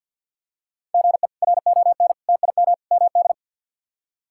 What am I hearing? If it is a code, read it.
Morse code, heard as a keyed tone, about 48 wpm, 700 Hz. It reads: GE RON TIM MD
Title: K1USN SST QSO Practice - 50wpm